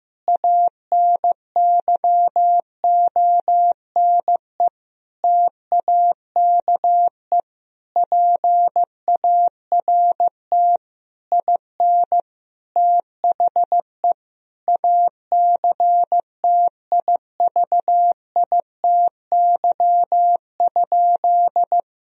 Morse code, heard as a keyed tone, about 16 wpm, 700 Hz. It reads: ANYONE TAKE PART IN THE ACTIVITY?
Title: Sentences from Top 1000 Words - Review Part 4 - 15wpm